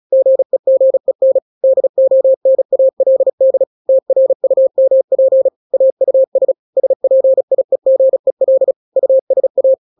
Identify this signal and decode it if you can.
Morse code, heard as a keyed tone, about 36 wpm, 530 Hz. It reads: GEGEN DONALD TRUMP AUS SPIEGEL USA
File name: daily / 35wpm